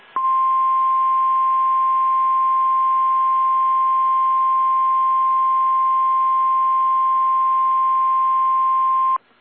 Emergency Alert System (SAME) tones (the 1050 Hz attention tone). It is heard from 0.4 to 8.7 s.